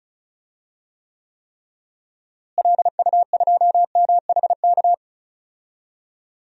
Morse code, heard as a keyed tone, about 36 wpm, 700 Hz. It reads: LU2MHK
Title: DXCC-100 Call Signs - 6 Characters - Repeated 3x - 35wpm